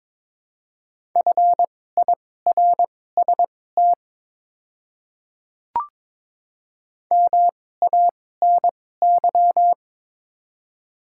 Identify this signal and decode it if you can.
Morse code, heard as a keyed tone, about 22 wpm, spaced as for 16 wpm, 690 Hz. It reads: FIRST MANY
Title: 2x Character Spacing - Top 100 Words - 22wpm